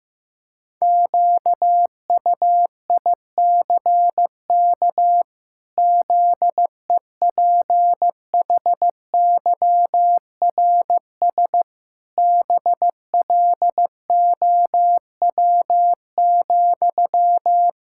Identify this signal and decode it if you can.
Morse code, heard as a keyed tone, about 15 wpm, 700 Hz. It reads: QUICK ZEPHYRS BLOW,
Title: Pangram Challenge v2 - 15wpm